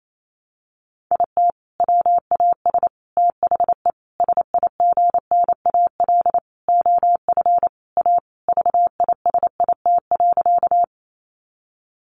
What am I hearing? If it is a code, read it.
Morse code, heard as a keyed tone, about 28 wpm, 700 Hz. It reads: IT WAS THE SIGNAL OF A VISIT.
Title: Sentences from Top 1400 Words 28wpm